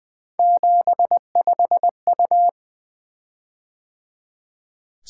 Morse code, heard as a keyed tone, about 20 wpm, 700 Hz. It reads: 75U
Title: Three Letter-Number Combinations - 20wpm